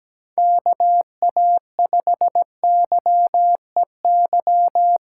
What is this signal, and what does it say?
Morse code, heard as a keyed tone, about 17 wpm, 700 Hz. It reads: KA5YEY
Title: USA Callsign DE Callsign 17wpm